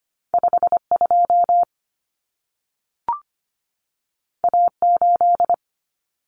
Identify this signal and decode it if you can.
Morse code, heard as a keyed tone, about 25 wpm, 700 Hz. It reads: 52 A8
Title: Common Cut Numbers - 2-Digit 25wpm